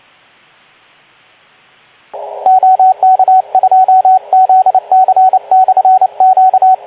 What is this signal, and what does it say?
Morse code, heard as a keyed tone, about 28 wpm, 750 Hz, with radio noise. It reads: OK2ZC/Q
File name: Archiv / CUC189 / ok2zcq cuc 189